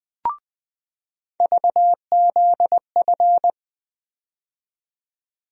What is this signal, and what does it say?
Morse code, heard as a keyed tone, about 20 wpm, 700 Hz. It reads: VZF
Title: USA Call Sign Postfixes - 3 Letters 20wpm